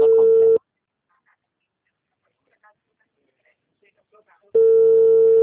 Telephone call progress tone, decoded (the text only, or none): ringback tone